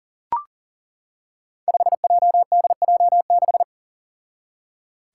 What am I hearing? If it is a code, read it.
Morse code, heard as a keyed tone, about 40 wpm, 700 Hz. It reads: 5JDJ6
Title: Five Challenging Letter-Number Combinations 40wpm